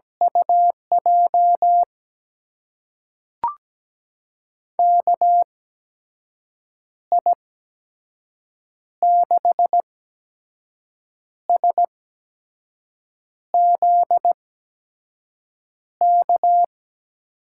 Morse code, heard as a keyed tone, about 17 wpm, spaced as for 5 wpm, 700 Hz. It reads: UJ KI6SZK
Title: USA Call Signs - 6 Characters - 8x Character Spacing - 17wpm